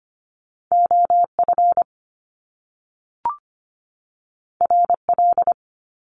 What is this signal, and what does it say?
Morse code, heard as a keyed tone, about 25 wpm, 700 Hz. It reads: OF RL
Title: Top Two Letter Combinations in Words 25wpm